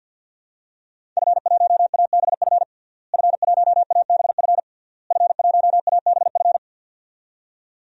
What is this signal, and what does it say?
Morse code, heard as a keyed tone, about 52 wpm, 700 Hz. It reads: F1ABF F1ABF F1ABF
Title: DXCC-100 Call Signs - 5 Characters - Repeated 3x - 50wpm